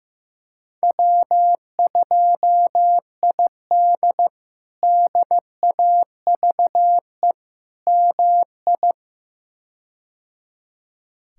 Morse code, heard as a keyed tone, about 15 wpm, 700 Hz. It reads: W2ID DAVE MI
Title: K1USN SST QSO Practice - 15wpm